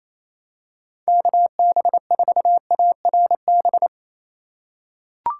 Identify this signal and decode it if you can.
Morse code, heard as a keyed tone, about 28 wpm, 700 Hz. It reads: KB4ARB
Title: USA Call Signs - 6 Characters - 2x Character Spacing - 28wpm